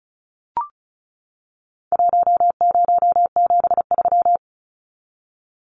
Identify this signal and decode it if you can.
Morse code, heard as a keyed tone, about 34 wpm, 700 Hz. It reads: 1073
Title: Numbers - 4-Digit 35wpm